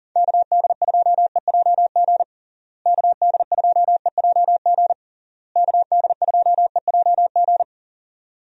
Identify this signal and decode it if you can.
Morse code, heard as a keyed tone, about 40 wpm, 700 Hz. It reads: KD2EJG KD2EJG KD2EJG
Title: USA Call Signs - 6 Characters - Repeated 3x - 40wpm